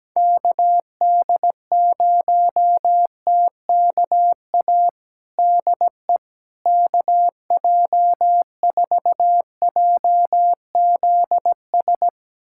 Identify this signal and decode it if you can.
Morse code, heard as a keyed tone, about 17 wpm, 700 Hz. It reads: KD0TKA DE KJ4JZS
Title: USA Callsign DE Callsign - 6 Characters - 17wpm